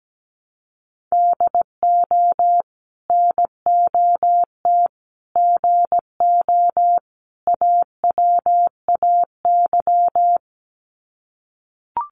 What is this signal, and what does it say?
Morse code, heard as a keyed tone, about 17 wpm, 700 Hz. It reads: DO NOT GO AWAY
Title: Sets of 4 Words 17wpm